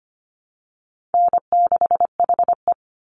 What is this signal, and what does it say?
Morse code, heard as a keyed tone, about 25 wpm, 700 Hz. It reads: N6HE
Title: Easy Call Signs Repeated 3x - 25wpm